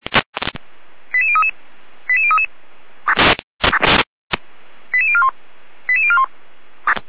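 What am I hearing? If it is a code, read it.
Five-tone selective call (ZVEI), heard as quick runs of five tones, tones of about 70 ms. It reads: 88033 88033 88041 88041
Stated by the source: ZVEI five-tone selective call 88191, received by SDR.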